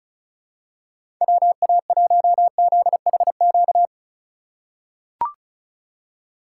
Morse code, heard as a keyed tone, about 35 wpm, 700 Hz. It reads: WA1ZHQ
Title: USA Call Signs - 6 Characters - 3x Character Spacing - 35wpm